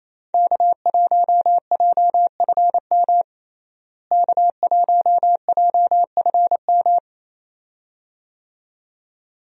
Morse code, heard as a keyed tone, about 28 wpm, 700 Hz. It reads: K1JFM K1JFM
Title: USA Call Signs - 5 Characters - Repeated 2x - 28wpm